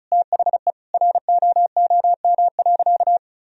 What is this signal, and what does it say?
Morse code, heard as a keyed tone, about 34 wpm, 700 Hz. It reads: THE ROOM.
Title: Sentences from Top 1000 Words - Review Part 4 - 35wpm